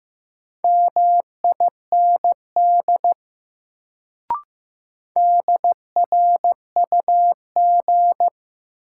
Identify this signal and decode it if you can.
Morse code, heard as a keyed tone, about 15 wpm, 700 Hz. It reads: MIND DRUG
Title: Sets of 5 Words - New Word Drill - 15wpm